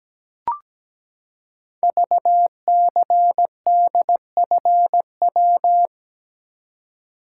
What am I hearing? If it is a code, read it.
Morse code, heard as a keyed tone, about 17 wpm, 700 Hz. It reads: VCDFW
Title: Code Groups - 5 Letters - 17wpm